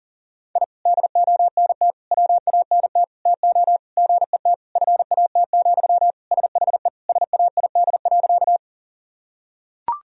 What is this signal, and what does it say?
Morse code, heard as a keyed tone, about 37 wpm, 700 Hz. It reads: I DONT WANT TO GET FAT, SHE SAID.
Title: Sentences from Top 1000 Words - Review Part 4 - 40wpm